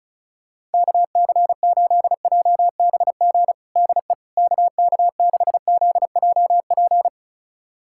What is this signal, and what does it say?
Morse code, heard as a keyed tone, about 36 wpm, 700 Hz. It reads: KC8JBG DE KK6ZJP
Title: USA Callsign DE Callsign - 6 Characters - 35wpm